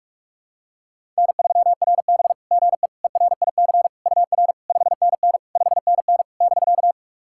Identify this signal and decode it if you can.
Morse code, heard as a keyed tone, about 44 wpm, 690 Hz. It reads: N3RB GE ERIC UR 5NN 5NN <BK>
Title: SOTA QSO Practice 45wpm